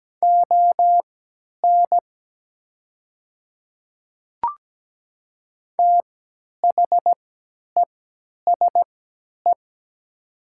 Morse code, heard as a keyed tone, about 17 wpm, spaced as for 10 wpm, 700 Hz. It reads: ON THESE
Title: 3x Character Spacing - Top 100 Words - 17wpm